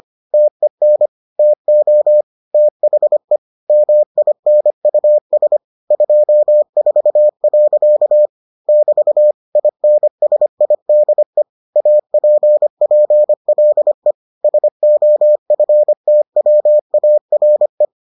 Morse code, heard as a keyed tone, about 25 wpm, 600 Hz. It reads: TEN TO THE MINUS 24. <BT> INSIDE APPLE SOFTWARE